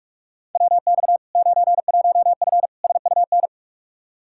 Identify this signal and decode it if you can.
Morse code, heard as a keyed tone, about 41 wpm, 700 Hz. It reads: WX 91F SUN
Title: Practice QSO - Exchange 2 - Info - 45wpm